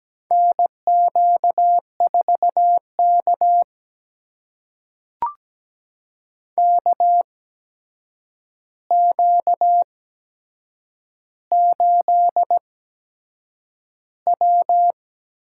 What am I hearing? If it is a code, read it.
Morse code, heard as a keyed tone, about 17 wpm, spaced as for 7 wpm, 700 Hz. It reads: NQ4K KQ8W
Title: USA Call Signs - 4 Characters - 8x Character Spacing - 17wpm